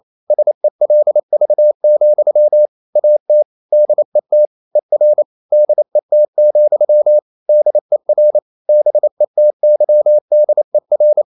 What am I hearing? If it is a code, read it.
Morse code, heard as a keyed tone, about 28 wpm, 600 Hz. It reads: SELV, AT DET ER DET, DER BETYDER
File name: Ferie QRQ 28wpm